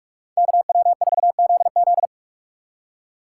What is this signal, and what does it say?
Morse code, heard as a keyed tone, about 55 wpm, 700 Hz. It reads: KW4ZZ
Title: Speed Racing - USA 2x2 Call Signs 45wpm